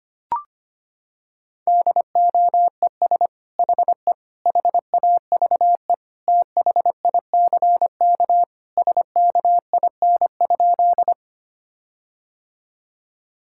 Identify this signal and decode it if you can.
Morse code, heard as a keyed tone, about 25 wpm, 700 Hz. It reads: DOES HE HAVE THICK SKIN?
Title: Sentences from Top 1600 Words 25wpm